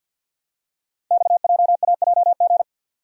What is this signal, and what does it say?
Morse code, heard as a keyed tone, about 48 wpm, 700 Hz. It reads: XJAJG
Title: Code Groups - 5 Letters - 50wpm